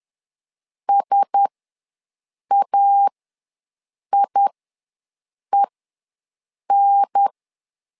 Morse code, heard as a keyed tone, about 10 wpm, spaced as for 6 wpm, 800 Hz. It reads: SAIEN
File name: LEZIONI / 4 Q1S ws 10wpm 2208